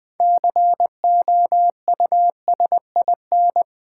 Morse code, heard as a keyed tone, about 20 wpm, 700 Hz. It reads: COUSIN
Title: Six Letter Words - 20wpm